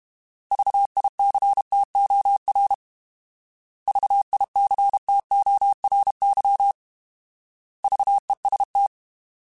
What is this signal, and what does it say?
Morse code, heard as a keyed tone, about 32 wpm, 800 Hz. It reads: VICTOR VICTORY VEST
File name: Alphabetical / Alphabetical mp3s / Vs @30